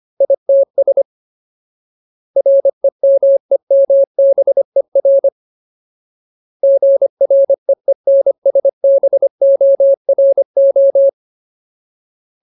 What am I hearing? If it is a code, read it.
Morse code, heard as a keyed tone, about 25 wpm, 550 Hz. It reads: ITS REMEMBER GREENSBORO
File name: Ultimate Word List / 25WPM Ultimate